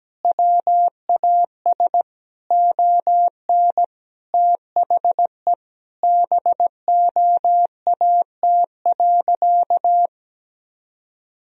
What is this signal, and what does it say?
Morse code, heard as a keyed tone, about 17 wpm, 700 Hz. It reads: WAS ON THE BOAT.
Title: Sentences from Top 1000 Words - Review Part 3 - 17wpm